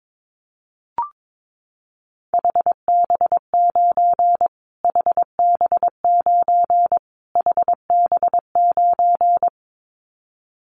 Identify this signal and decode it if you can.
Morse code, heard as a keyed tone, about 22 wpm, 700 Hz. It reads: HB9 HB9 HB9
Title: Callsign Prefixes for Top 25 DXCC - Repeated 3x - 22wpm